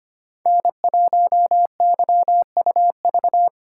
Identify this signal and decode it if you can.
Morse code, heard as a keyed tone, about 25 wpm, 700 Hz. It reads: N1YUV
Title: USA Call Signs - 5 Characters - 3x Character Spacing - 25wpm